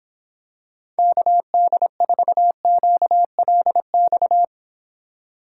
Morse code, heard as a keyed tone, about 26 wpm, 700 Hz. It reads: KD4QLX